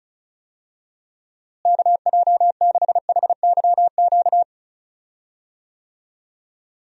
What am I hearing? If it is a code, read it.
Morse code, heard as a keyed tone, about 34 wpm, 700 Hz. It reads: KJ6HYQ
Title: USA Call Signs - 6 Characters - Repeated 2x - 35wpm